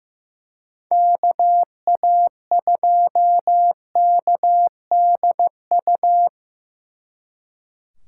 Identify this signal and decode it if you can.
Morse code, heard as a keyed tone, about 18 wpm, 700 Hz. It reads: KA2KDU
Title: Speed Racing - USA 2x3 Call Signs 15wpm